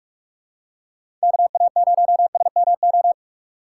Morse code, heard as a keyed tone, about 46 wpm, 700 Hz. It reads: KA0SMO
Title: USA Call Signs - 6 Characters - 2x Character Spacing - 45wpm